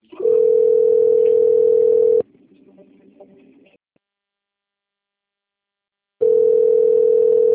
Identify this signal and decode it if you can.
Ringback tone (North America), 440 + 480 Hz.
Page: Itatiba 2